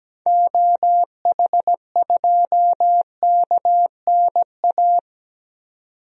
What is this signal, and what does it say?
Morse code, heard as a keyed tone, about 17 wpm, 700 Hz. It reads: OH2KNA
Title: DXCC-100 Call Signs - 6 Characters - 17wpm